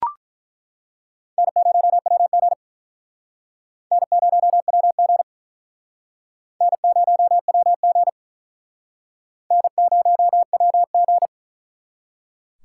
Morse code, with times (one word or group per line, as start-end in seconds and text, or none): 1.4-2.5 N0WG
3.9-5.2 N0WG
6.6-8.1 N0WG
9.5-11.3 N0WG